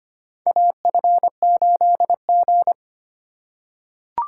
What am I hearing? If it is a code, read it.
Morse code, heard as a keyed tone, about 25 wpm, 700 Hz. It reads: AF8G
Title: USA Call Signs - 4 Characters - 3x Character Spacing - 25wpm